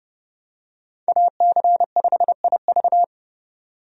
Morse code, heard as a keyed tone, about 31 wpm, 700 Hz. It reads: AC5IV
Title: USA Call Signs - 5 Characters - 30wpm